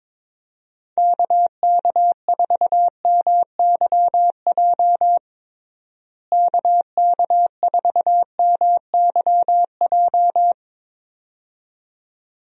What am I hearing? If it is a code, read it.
Morse code, heard as a keyed tone, about 22 wpm, 700 Hz. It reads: KK4MYJ KK4MYJ
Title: USA Call Signs - 6 Characters - Repeated 2x - 22wpm